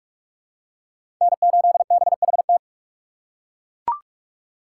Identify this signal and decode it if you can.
Morse code, heard as a keyed tone, about 40 wpm, 700 Hz. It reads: N8BHT